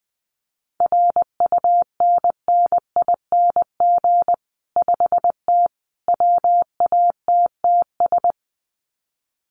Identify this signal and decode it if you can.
Morse code, heard as a keyed tone, about 20 wpm, 700 Hz. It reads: RUNNING 5T WATTS
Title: Common QSO Phrases 20wpm